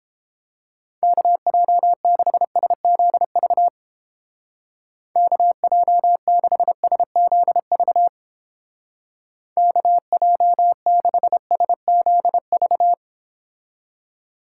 Morse code, heard as a keyed tone, about 30 wpm, 700 Hz. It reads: KJ6SZV KJ6SZV KJ6SZV